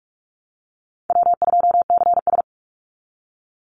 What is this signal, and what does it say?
Morse code, heard as a keyed tone, about 44 wpm, 700 Hz. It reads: W2CS
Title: Easy Call Signs Repeated 3x - 45wpm